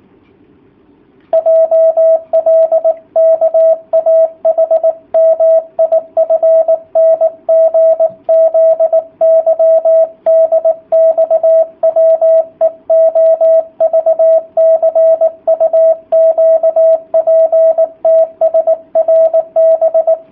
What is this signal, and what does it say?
Morse code, heard as a keyed tone, about 19 wpm, 650 Hz, with radio noise. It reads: JLKAHMIFNGZYDXWEOVCUQPTSRB
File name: jh8cbh / 2019cw / sp18- w